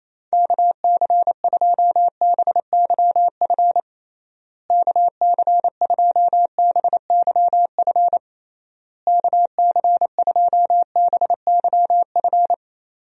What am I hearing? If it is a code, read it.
Morse code, heard as a keyed tone, about 28 wpm, 700 Hz. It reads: KC2BYF KC2BYF KC2BYF
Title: USA Call Signs - 6 Characters - Repeated 3x - 28wpm